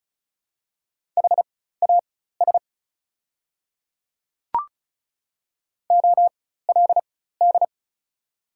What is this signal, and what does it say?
Morse code, heard as a keyed tone, about 36 wpm, spaced as for 16 wpm, 690 Hz. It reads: HAS OLD
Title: 4x Character Spacing - Top 300 Words - 35wpm